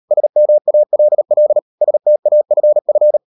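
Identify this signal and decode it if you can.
Morse code, heard as a keyed tone, about 38 wpm, 600 Hz. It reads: SMALL STAFF
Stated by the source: Morse code practice at 38 WPM, sending Ken Follett's novel Never in English, from CW med Gnister.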